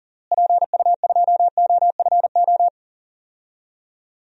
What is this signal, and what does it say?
Morse code, heard as a keyed tone, about 40 wpm, 700 Hz. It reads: PU2OFO
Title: DXCC-100 Call Signs - 6 Characters - Repeated 3x - 40wpm